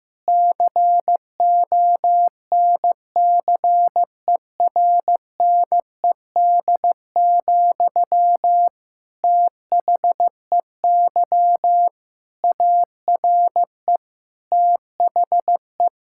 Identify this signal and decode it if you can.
Morse code, heard as a keyed tone, about 15 wpm, 700 Hz. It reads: CONCERNED, THEY ARE THE
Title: Sentences from Top 1000 Words - Review Part 6 - 15wpm